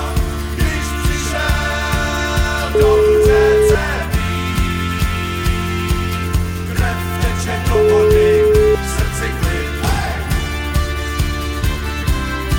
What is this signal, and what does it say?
Ringback tone (Europe, 425 Hz), with noise.